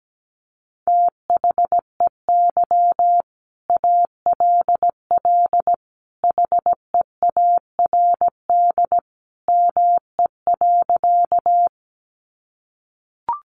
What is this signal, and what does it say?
Morse code, heard as a keyed tone, about 17 wpm, 700 Hz. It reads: THEY ALL HEARD ME.
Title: Sentences from Top 400 Words 17wpm